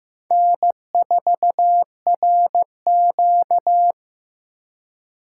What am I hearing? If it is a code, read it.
Morse code, heard as a keyed tone, about 18 wpm, 700 Hz. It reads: N4RQ
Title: Speed Racing - USA 1x2 Call Signs 15wpm